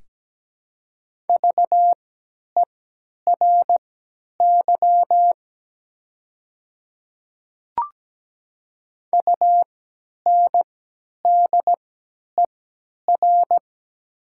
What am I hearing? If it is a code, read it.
Morse code, heard as a keyed tone, about 17 wpm, spaced as for 10 wpm, 700 Hz. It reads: VERY UNDER
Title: 3x Character Spacing - Top 200 Words - 17wpm